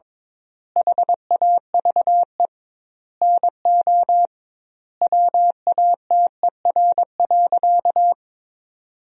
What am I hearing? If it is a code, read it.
Morse code, heard as a keyed tone, about 21 wpm, 700 Hz. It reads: HAVE NO WATER.
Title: 2x Word-Spacing - Sentences from Top 100 Words - 22wpm